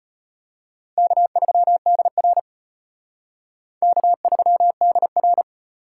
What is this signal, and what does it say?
Morse code, heard as a keyed tone, about 32 wpm, 700 Hz. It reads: K3DR K3DR